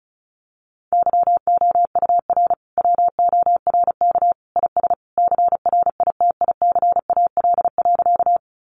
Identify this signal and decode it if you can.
Morse code, heard as a keyed tone, about 34 wpm, 700 Hz. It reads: YOUR WORK IS CRITICAL.
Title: Sentences from Top 1100 Words 35wpm